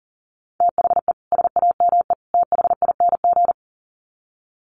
Morse code, heard as a keyed tone, about 40 wpm, 700 Hz. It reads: THE SAME THING
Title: Sets of 3 Words 40wpm